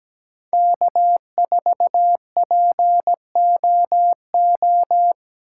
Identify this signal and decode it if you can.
Morse code, heard as a keyed tone, about 17 wpm, 700 Hz. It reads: K4POO
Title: USA Call Signs - 5 Characters - 4x Character Spacing - 17wpm